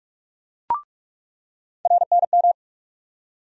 Morse code, heard as a keyed tone, about 44 wpm, 700 Hz. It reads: RNM
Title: USA Call Sign Postfixes - 3 Letters 45wpm